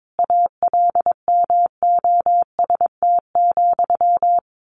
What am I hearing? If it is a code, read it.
Morse code, heard as a keyed tone, about 22 wpm, 700 Hz. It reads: ALMOST,
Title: A Sound Of Thunder 22wpm